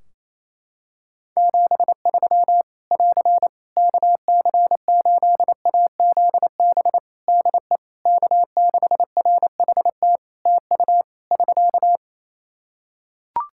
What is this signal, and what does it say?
Morse code, heard as a keyed tone, about 27 wpm, 700 Hz. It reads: 73 <AR> KC8AZB DE K6RHT TU <SK>
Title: Practice QSO - Exchange 4 - Ending - 28wpm